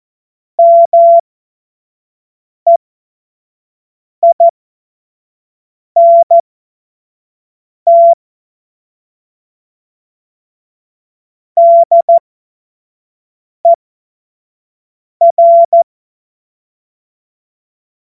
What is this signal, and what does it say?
Morse code, heard as a keyed tone, about 14 wpm, spaced as for 5 wpm, 680 Hz. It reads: MEINT DER
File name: Morsekurs / Witze / Witz 21. März 2023 25 F70